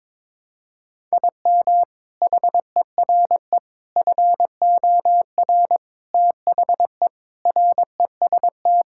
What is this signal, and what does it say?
Morse code, heard as a keyed tone, about 22 wpm, 700 Hz. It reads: IM HERE FOR THE REST
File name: practice / 22wpm / Sentences with 4-Word Phrases - 22wpm